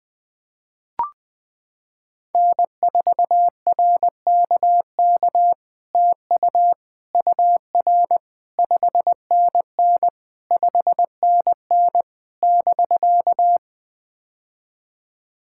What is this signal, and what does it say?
Morse code, heard as a keyed tone, about 20 wpm, 700 Hz. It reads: N4RKK TU UR 5NN 5NN <BK>
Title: POTA QSO Practice - 20wpm